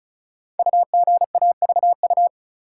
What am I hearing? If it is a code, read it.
Morse code, heard as a keyed tone, about 36 wpm, 700 Hz. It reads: UGAVU
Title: Code Groups - 5 Letter:Numbers - 35wpm